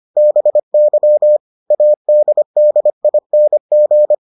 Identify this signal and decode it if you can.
Morse code, heard as a keyed tone, about 25 wpm, 600 Hz. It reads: BY ADDING